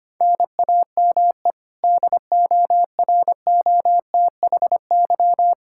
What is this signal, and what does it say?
Morse code, heard as a keyed tone, about 25 wpm, 700 Hz. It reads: NAME DOROTHY
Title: Practice QSO - Exchange 1 - Beginning - 25wpm